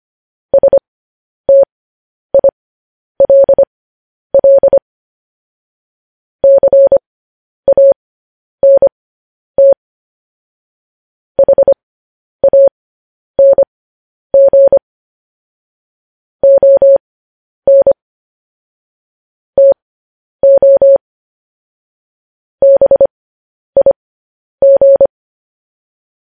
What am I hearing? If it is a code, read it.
Morse code, heard as a keyed tone, about 25 wpm, spaced as for 10 wpm, 550 Hz. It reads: STILL CANT HANG ON TO BIG
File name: podcast / News headlines-1x-25x10